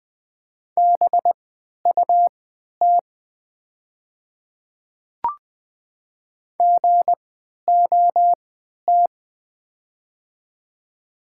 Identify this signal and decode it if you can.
Morse code, heard as a keyed tone, about 20 wpm, spaced as for 11 wpm, 700 Hz. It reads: BUT GOT
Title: 3x Character Spacing - Top 300 Words - 20wpm